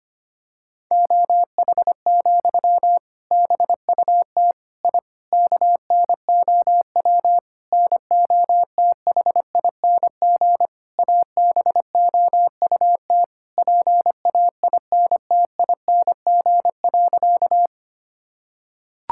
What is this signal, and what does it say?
Morse code, heard as a keyed tone, about 25 wpm, 700 Hz. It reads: OH, BUT I KNOW NOTHING ABOUT PAINTING.
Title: Sentences from Top 1000 Words - Review Part 6 - 25wpm